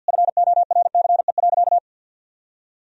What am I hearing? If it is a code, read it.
Morse code, heard as a keyed tone, about 51 wpm, 700 Hz. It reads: FORCE.
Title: Sentences from Top 1000 Words - Review Part 3 - 50wpm